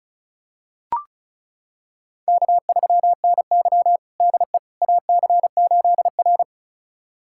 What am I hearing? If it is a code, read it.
Morse code, heard as a keyed tone, about 36 wpm, 700 Hz. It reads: K3NY DE AC8R
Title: USA Callsign DE Callsign - 4 Characters - 35wpm